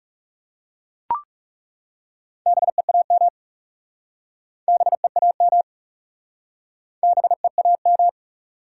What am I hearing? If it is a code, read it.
Morse code, heard as a keyed tone, about 40 wpm, 700 Hz. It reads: BEAM BEAM BEAM